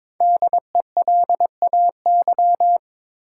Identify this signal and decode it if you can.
Morse code, heard as a keyed tone, about 22 wpm, 700 Hz. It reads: DELAY
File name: 5 Letter Words - First 3 Letters Given - 22wpm